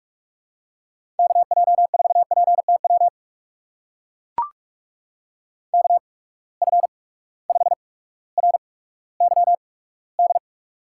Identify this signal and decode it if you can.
Morse code, heard as a keyed tone, about 44 wpm, spaced as for 19 wpm, 700 Hz. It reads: KJ4PTW KF5RYD
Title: USA Call Signs - 6 Characters - 8x Character Spacing - 45wpm